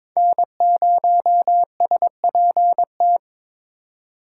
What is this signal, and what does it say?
Morse code, heard as a keyed tone, about 26 wpm, 700 Hz. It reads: N0SPT
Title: Speed Racing - USA 1x3 Call Signs 22wpm